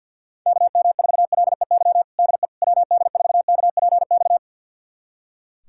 Morse code, heard as a keyed tone, about 45 wpm, 700 Hz. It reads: KM4LEY DE WD4KPX
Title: USA Callsign DE Callsign 50wpm